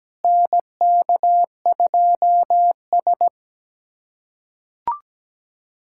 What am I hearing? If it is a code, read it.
Morse code, heard as a keyed tone, about 17 wpm, 700 Hz. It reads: NK2S
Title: USA Callsign DE Callsign - 4 Characters - 17wpm